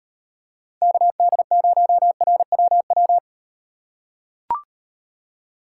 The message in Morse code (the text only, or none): KD0RWW